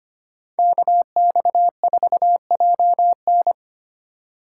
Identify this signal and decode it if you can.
Morse code, heard as a keyed tone, about 25 wpm, 700 Hz. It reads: KX4JN
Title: USA Call Signs - 4x Character Spacing - 25wpm